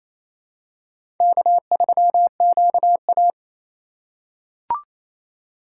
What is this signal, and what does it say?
Morse code, heard as a keyed tone, about 28 wpm, 700 Hz. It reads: K3QA
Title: USA Call Signs - 4 Characters - 3x Character Spacing - 28wpm